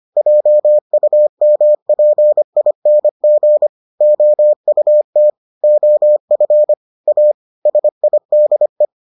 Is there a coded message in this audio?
Yes: Morse code from 0.2 to 8.9 s, heard as a keyed tone, about 25 wpm, 600 Hz. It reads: JUMPING OUT OF A SIDE